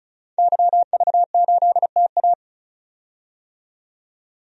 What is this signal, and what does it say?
Morse code, heard as a keyed tone, about 34 wpm, 700 Hz. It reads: YV8TA
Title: DXCC-100 Call Signs - 5 Characters - 35wpm